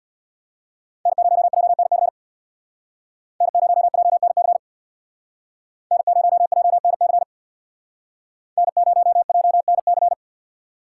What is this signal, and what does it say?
Morse code, heard as a keyed tone, about 58 wpm, 700 Hz. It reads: N0JNC N0JNC N0JNC N0JNC
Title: Speed Racing - USA Call Signs 50wpm